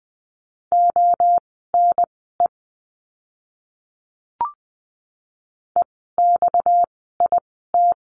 Morse code, heard as a keyed tone, about 20 wpm, spaced as for 14 wpm, 700 Hz. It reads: ONE EXIT
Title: 230 Lesson 37f - Letter x - Words 20wpm